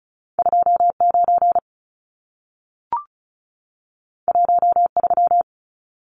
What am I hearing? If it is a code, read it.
Morse code, heard as a keyed tone, about 36 wpm, 700 Hz. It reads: 29 13
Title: Numbers - 2-Digit 35wpm